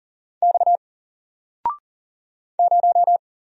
Morse code, heard as a keyed tone, about 40 wpm, 700 Hz. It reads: X 0
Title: Single Letter-Number 40wpm